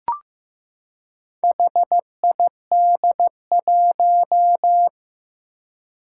Morse code, heard as a keyed tone, about 15 wpm, 700 Hz. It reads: HID1
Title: Four Challenging Letter-Number Combinations 15wpm